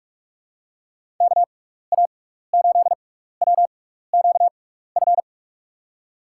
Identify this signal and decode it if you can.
Morse code, heard as a keyed tone, about 48 wpm, spaced as for 16 wpm, 700 Hz. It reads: KA8WQF
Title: USA Call Signs - 6 Characters - 6x Character Spacing - 45wpm